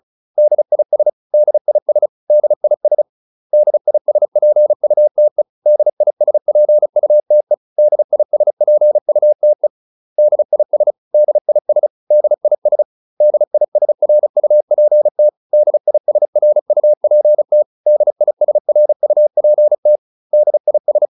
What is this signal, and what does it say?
Morse code, heard as a keyed tone, about 34 wpm, 600 Hz. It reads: DIS DIS DIS DISPUTE DISPUTE DISPUTE DIS DIS DIS DISRUPT DISRUPT DISRUPT DIS